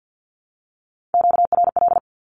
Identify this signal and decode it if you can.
Morse code, heard as a keyed tone, about 48 wpm, 700 Hz. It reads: QRL
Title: Q-Codes Repeated 3x 50wpm